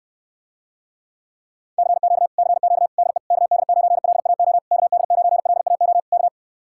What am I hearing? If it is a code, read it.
Morse code, heard as a keyed tone, about 70 wpm, 700 Hz. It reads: CQ CQ DE KD9LAO KD9LAO K